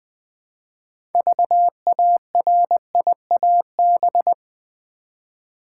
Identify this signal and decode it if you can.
Morse code, heard as a keyed tone, about 20 wpm, 700 Hz. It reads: VARIAB
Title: Build A Word - 2 Letters At A Time - 8-Letter Words - 20wpm